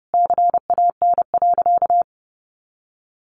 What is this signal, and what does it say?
Morse code, heard as a keyed tone, about 30 wpm, 700 Hz. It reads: CAN.
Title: Sentences from Top 500 Words 30wpm